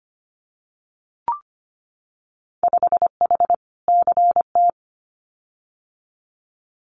Morse code, heard as a keyed tone, about 25 wpm, 700 Hz. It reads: 5H CT
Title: Winter Field Day Exchanges Encore 25wpm